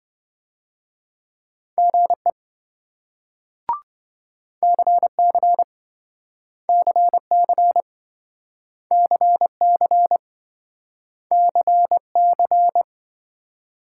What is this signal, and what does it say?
Morse code, heard as a keyed tone, about 26 wpm, 700 Hz. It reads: GE CC CC CC CC